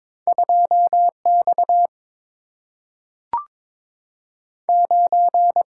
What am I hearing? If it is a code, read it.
Morse code, heard as a keyed tone, about 22 wpm, 700 Hz. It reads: 2 X 9
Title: Single-Letter Single-Number 22wpm